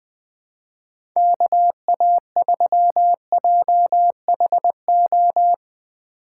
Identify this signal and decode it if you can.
Morse code, heard as a keyed tone, about 20 wpm, 700 Hz. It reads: KA3JHO
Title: USA Call Signs - 3x Character Spacing - 20wpm